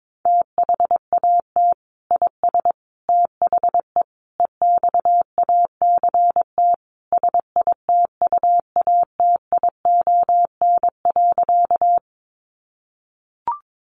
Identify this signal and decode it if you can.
Morse code, heard as a keyed tone, about 22 wpm, 700 Hz. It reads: THAT IS THE EXACT SITUATION.
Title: Sentences from Top 1300 Words 22wpm